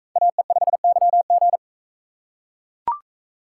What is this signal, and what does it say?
Morse code, heard as a keyed tone, about 42 wpm, 700 Hz. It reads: AE5YG